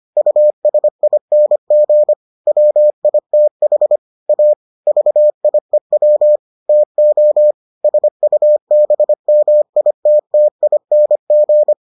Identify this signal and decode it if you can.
Morse code, heard as a keyed tone, about 25 wpm, 600 Hz. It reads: USING WITH A VIEW TO SUBMITTING